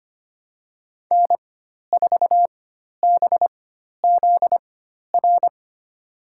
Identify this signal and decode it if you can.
Morse code, heard as a keyed tone, about 25 wpm, spaced as for 12 wpm, 700 Hz. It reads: N4BZR
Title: USA Call Signs - 5 Characters - 4x Character Spacing - 25wpm